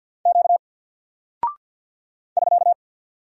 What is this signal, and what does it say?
Morse code, heard as a keyed tone, about 51 wpm, 700 Hz. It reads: Q <SK>
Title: Single Character (Letter:Number:Punctuation:Prosign) Alt 50wpm